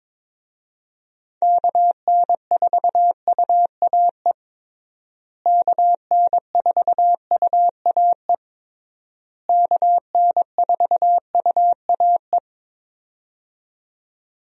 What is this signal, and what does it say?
Morse code, heard as a keyed tone, about 22 wpm, 700 Hz. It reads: KN4UAE KN4UAE KN4UAE
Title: USA Call Signs - 6 Characters - Repeated 3x - 22wpm